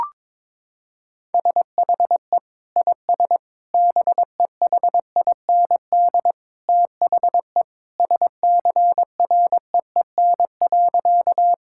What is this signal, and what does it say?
Morse code, heard as a keyed tone, about 22 wpm, 700 Hz. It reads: SHE IS BEHIND THE SCREEN.
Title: Sentences from Top 1000 Words - Review Part 6 - 22wpm